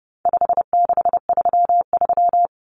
Morse code, heard as a keyed tone, about 29 wpm, 700 Hz. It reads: 5633
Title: Numbers - 4-Digit 30wpm